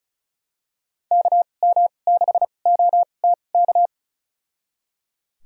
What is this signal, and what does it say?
Morse code, heard as a keyed tone, about 34 wpm, spaced as for 25 wpm, 700 Hz. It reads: KM6OTK
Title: USA Call Signs - 2x Character Spacing - 35wpm